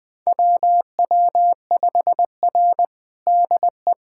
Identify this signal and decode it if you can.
Morse code, heard as a keyed tone, about 20 wpm, 700 Hz. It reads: WW5R DE
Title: USA Callsign DE Callsign - 4 Characters - 20wpm